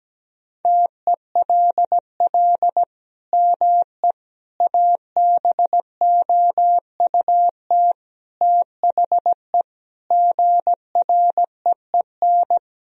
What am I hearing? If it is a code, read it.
Morse code, heard as a keyed tone, about 17 wpm, 700 Hz. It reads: TELL ME ABOUT THE GREEN
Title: Sentences from Top 1000 Words - Review Part 2 - 17wpm